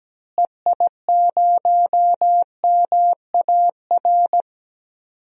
Morse code, heard as a keyed tone, about 17 wpm, 700 Hz. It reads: EI0MAR
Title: DXCC-100 Call Signs - 6 Characters - Repeated 3x - 17wpm